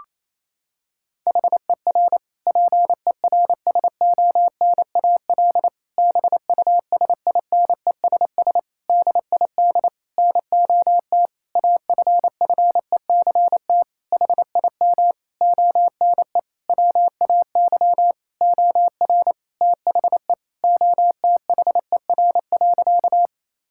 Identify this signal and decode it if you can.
Morse code, heard as a keyed tone, about 28 wpm, 700 Hz. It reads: HER PERSONAL BUSINESS DID NOT AFFECT HIM ONE WAY OR THE OTHER.
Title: Sentences from Top 1000 Words - Review Part 6 - 28wpm